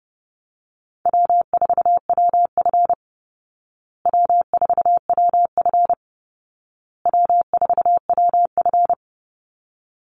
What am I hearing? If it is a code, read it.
Morse code, heard as a keyed tone, about 30 wpm, 700 Hz. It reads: W4WF W4WF W4WF
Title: Easy Call Signs Repeated 3x - 30wpm